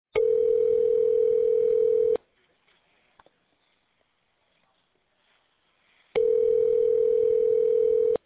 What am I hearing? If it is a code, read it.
Ringback tone (North America), 440 + 480 Hz.